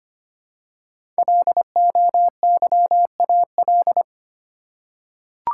Morse code, heard as a keyed tone, about 25 wpm, 700 Hz. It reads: LOYAL
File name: 5 Letter Words - First 3 Letters Given - 25wpm